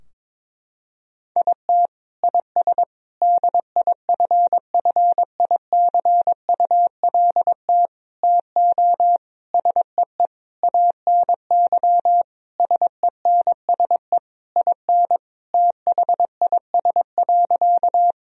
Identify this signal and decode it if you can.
Morse code, heard as a keyed tone, about 22 wpm, 700 Hz. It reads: IT IS DIFFICULT TO SEE ANY SENSE IN THIS.
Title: Sentences from Top 1000 Words - Review Part 4 - 22wpm